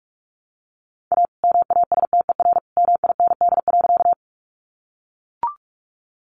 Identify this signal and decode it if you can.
Morse code, heard as a keyed tone, about 44 wpm, 700 Hz. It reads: A MASTER MIND.
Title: Sentences from Top 1300 Words 45wpm